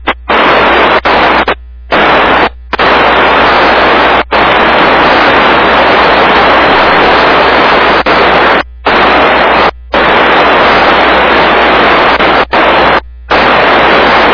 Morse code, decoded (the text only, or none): none